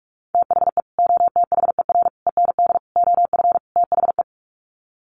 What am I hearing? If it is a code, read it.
Morse code, heard as a keyed tone, about 44 wpm, 700 Hz. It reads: THE OTHER END OF THE
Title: Sets of 5 Words 45wpm